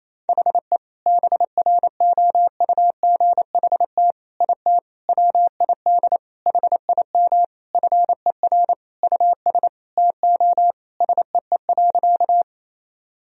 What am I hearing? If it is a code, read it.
Morse code, heard as a keyed tone, about 28 wpm, 700 Hz. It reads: HE BROUGHT IT WID HIM FER US TO SEE.
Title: Sentences from Top 500 Words and Abbreviations 28wpm